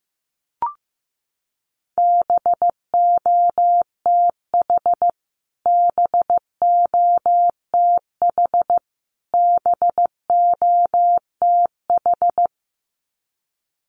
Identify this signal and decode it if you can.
Morse code, heard as a keyed tone, about 15 wpm, 700 Hz. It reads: BOTH BOTH BOTH
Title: Top 500 Words Repeated 15wpm